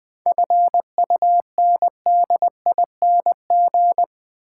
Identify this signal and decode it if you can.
Morse code, heard as a keyed tone, about 21 wpm, 700 Hz. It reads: FUNDING
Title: 7 Letter Words - First 3 Letters Given - 20wpm